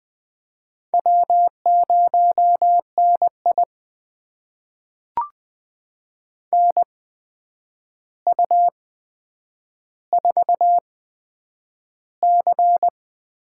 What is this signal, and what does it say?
Morse code, heard as a keyed tone, about 20 wpm, spaced as for 9 wpm, 700 Hz. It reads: W0NI NU4C
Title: USA Call Signs - 4 Characters - 8x Character Spacing - 20wpm